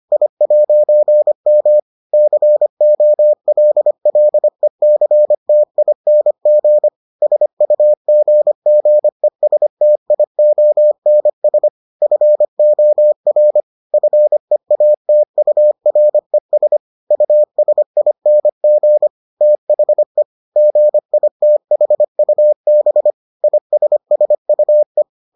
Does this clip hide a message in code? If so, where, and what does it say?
Yes: Morse code from 0.1 to 25.0 s, heard as a keyed tone, about 25 wpm, 600 Hz. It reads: I'M COLLECTING SUGGESTIONS FOR FEATURES USING THE GITHUB ISSUE